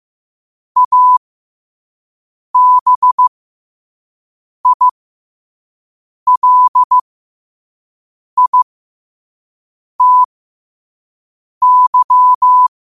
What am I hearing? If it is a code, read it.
Morse code, heard as a keyed tone, about 15 wpm, spaced as for 5 wpm, 1000 Hz. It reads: ABILITY
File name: qotdfiles / 5wpm